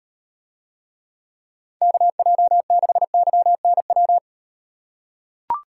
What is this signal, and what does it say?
Morse code, heard as a keyed tone, about 38 wpm, 700 Hz. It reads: KJ6YNW